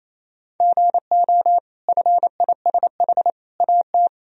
Morse code, heard as a keyed tone, about 27 wpm, 700 Hz. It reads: GO FISH AT
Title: Sentences from Top 1000 Words - Review Part 1 - 28wpm